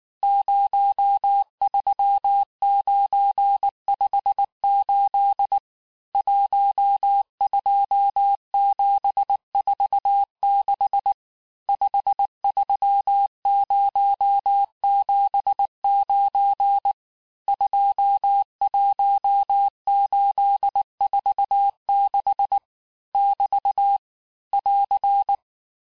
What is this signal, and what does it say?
Morse code, heard as a keyed tone, about 19 wpm, 800 Hz. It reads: 03958 12746 53079 21846 <BT> <AR>
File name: cours-du-lundi files / cours-1595-teste-CHIFFRE-vitesse18-MPM